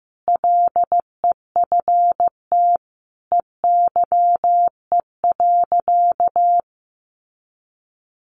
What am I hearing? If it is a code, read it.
Morse code, heard as a keyed tone, about 15 wpm, 700 Hz. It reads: LEFT EYE.
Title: Sentences from Top 1400 Words 15wpm